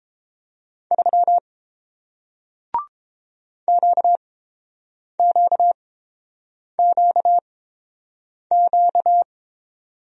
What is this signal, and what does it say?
Morse code, heard as a keyed tone, about 29 wpm, spaced as for 7 wpm, 690 Hz. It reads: 3 QQQQ